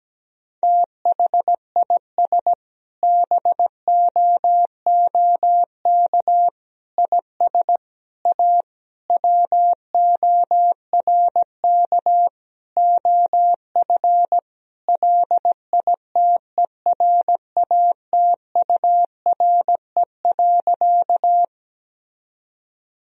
Morse code, heard as a keyed tone, about 17 wpm, 700 Hz. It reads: THIS BOOK IS A WORK OF LITERATURE.
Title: Sentences from Top 1000 Words - Review Part 3 - 17wpm